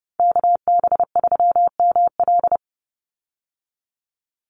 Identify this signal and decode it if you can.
Morse code, heard as a keyed tone, about 30 wpm, 700 Hz. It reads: KB3ML
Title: Call Signs Repeated 2x - 30wpm